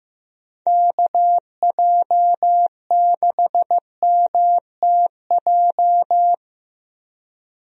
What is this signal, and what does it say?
Morse code, heard as a keyed tone, about 15 wpm, 700 Hz. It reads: KJ6MTJ
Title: USA Call Signs - 6 Characters - Repeated 3x - 15wpm